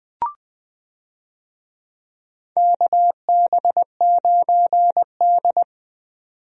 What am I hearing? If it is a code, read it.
Morse code, heard as a keyed tone, about 20 wpm, 700 Hz. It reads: KB9D
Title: USA Call Signs - 4 Characters - 20wpm